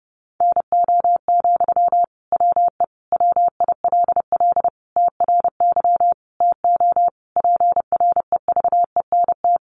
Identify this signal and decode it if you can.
Morse code, heard as a keyed tone, about 30 wpm, 700 Hz. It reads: NO, WE WILL TRY TO PREVENT
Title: Sentences from Top 1100 Words 30wpm